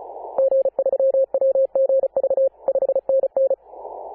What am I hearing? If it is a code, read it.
Morse code, heard as a keyed tone, about 34 wpm, 540 Hz. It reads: G3WGV 5NN
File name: Apr/29 2042z JD1BLY 14.011MHz CW